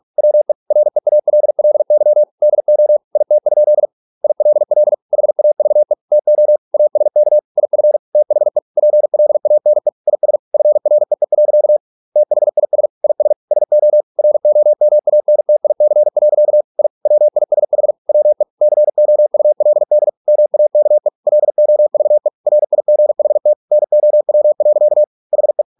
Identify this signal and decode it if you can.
Morse code, heard as a keyed tone, about 46 wpm, 600 Hz. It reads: WE REALLY DO IT? ILL HAVE TO ASK IF THE PLANE IS FREE. THIS IS SO ROMANTIC. I WISH WE COULD MAKE LOVE RIGHT NOW. HE